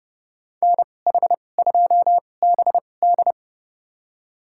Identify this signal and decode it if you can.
Morse code, heard as a keyed tone, about 30 wpm, spaced as for 22 wpm, 700 Hz. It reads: NH2BD
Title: USA Call Signs - 5 Characters - 2x Character Spacing - 30wpm